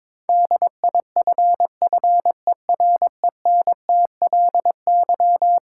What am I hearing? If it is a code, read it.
Morse code, heard as a keyed tone, about 22 wpm, 700 Hz. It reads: DIFFERENTLY
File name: practice / 22wpm / Eleven Letter Words - 22wpm